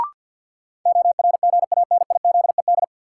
Morse code, heard as a keyed tone, about 51 wpm, 700 Hz. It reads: ORGANIZED
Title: Sentences from Top 1300 Words - New Word Drill Alt - 50wpm